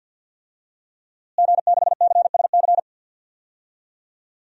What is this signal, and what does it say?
Morse code, heard as a keyed tone, about 47 wpm, 700 Hz. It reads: G6CSC
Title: DXCC-100 Call Signs - 5 Characters - Repeated 3x - 50wpm